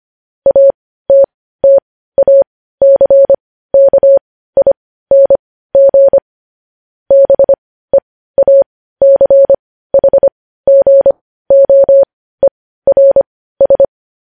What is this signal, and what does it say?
Morse code, heard as a keyed tone, about 25 wpm, spaced as for 15 wpm, 550 Hz. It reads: ATTACKING BEACHGOERS
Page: news-headlines-in-morse-code-at-15-wpm